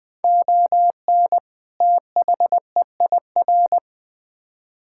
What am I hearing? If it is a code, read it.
Morse code, heard as a keyed tone, about 20 wpm, 700 Hz. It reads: ON THEIR
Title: Build A Phrase - 2-Word Phrases - 20wpm